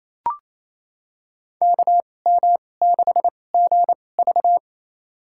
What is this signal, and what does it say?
Morse code, heard as a keyed tone, about 28 wpm, spaced as for 20 wpm, 700 Hz. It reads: KM6GV
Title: USA Call Signs - 5 Characters - 2x Character Spacing - 28wpm